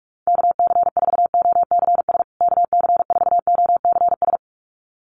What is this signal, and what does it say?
Morse code, heard as a keyed tone, about 44 wpm, 700 Hz. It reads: KC4OCS KC4OCS
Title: USA Call Signs Repeated 2x 45wpm